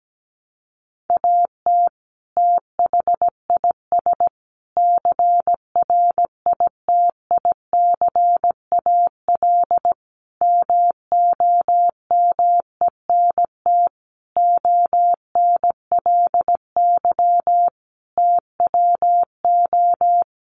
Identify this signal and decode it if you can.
Morse code, heard as a keyed tone, about 17 wpm, 700 Hz. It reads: AT THIS CRITICAL MOMENT ONLY TWO
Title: Sentences from Top 1100 Words 17wpm